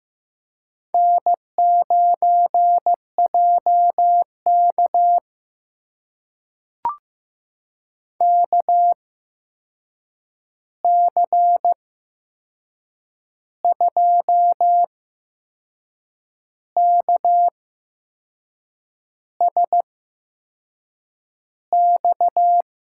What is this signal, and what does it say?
Morse code, heard as a keyed tone, about 15 wpm, spaced as for 6 wpm, 700 Hz. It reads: N9JK KC2KSX
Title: USA Call Signs - 8x Character Spacing - 15wpm